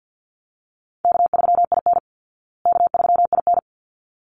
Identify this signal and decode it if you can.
Morse code, heard as a keyed tone, about 49 wpm, 700 Hz. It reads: K3IN K3IN
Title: Call Signs Repeated 3x - 50wpm